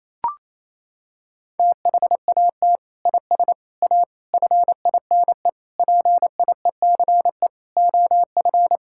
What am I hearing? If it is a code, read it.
Morse code, heard as a keyed tone, about 28 wpm, 700 Hz. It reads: THAT IS A FINE PIECE OF
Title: Sentences from Top 1000 Words - Review Part 5 - 28wpm